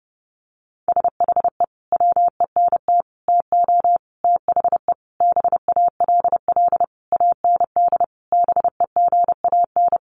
Morse code, heard as a keyed tone, about 30 wpm, 700 Hz. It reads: SHE WENT TO THE BALL AND BEGAN
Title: Sentences from Top 900 Words 30wpm